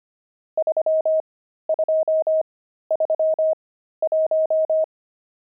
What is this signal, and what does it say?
Morse code, heard as a keyed tone, about 25 wpm, spaced as for 13 wpm, 630 Hz. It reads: 3231
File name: pangram-fs-13-cs-25-nums14